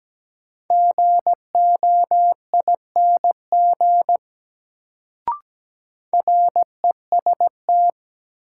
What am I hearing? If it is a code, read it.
Morse code, heard as a keyed tone, about 17 wpm, 700 Hz. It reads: GOING REST
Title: Sentences from Top 500 Words - New Word Drill - 17wpm